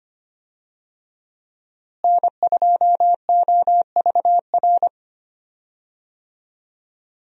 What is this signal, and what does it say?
Morse code, heard as a keyed tone, about 25 wpm, 700 Hz. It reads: N2OVR
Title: USA Call Signs - 5 Characters - Repeated 3x - 25wpm